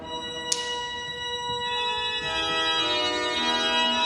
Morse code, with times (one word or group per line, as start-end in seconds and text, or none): none